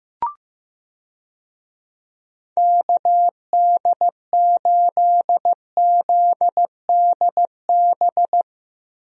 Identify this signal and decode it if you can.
Morse code, heard as a keyed tone, about 15 wpm, 700 Hz. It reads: KD8ZDB
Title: USA Call Signs - 6 Characters - Repeated 3x - 15wpm